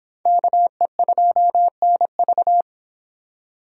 Morse code, heard as a keyed tone, about 26 wpm, 700 Hz. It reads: KE2NV